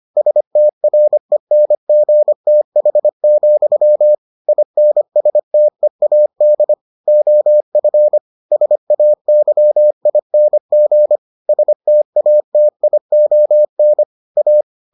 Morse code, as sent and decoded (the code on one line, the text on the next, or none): ... - .-. -. --. - .... --..-- / .. -. ... - . .- -.. / --- ..-. / ... .- -.-- .. -. --. / ... - .- - .. --- -. / .-
STRENGTH, INSTEAD OF SAYING STATION A